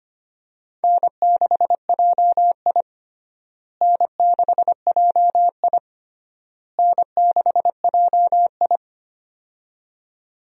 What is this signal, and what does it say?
Morse code, heard as a keyed tone, about 25 wpm, 700 Hz. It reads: N6JI N6JI N6JI
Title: USA Call Signs - 4 Characters - Repeated 3x - 25wpm